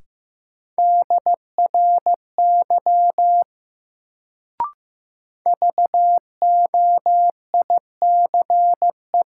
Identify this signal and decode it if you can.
Morse code, heard as a keyed tone, about 15 wpm, 700 Hz. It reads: DRY VOICE
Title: Sentences from Top 500 Words - New Word Drill - 15wpm